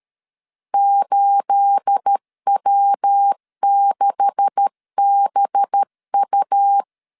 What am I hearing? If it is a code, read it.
Morse code, heard as a keyed tone, about 13 wpm, 800 Hz. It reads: 8W6BU
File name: LEZIONI / 9 286 12wpm 2305